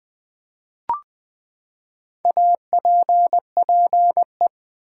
Morse code, heard as a keyed tone, about 20 wpm, 700 Hz. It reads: APPE
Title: Build A Word - 4 Letters At A Time - 8-Letter Words - 20wpm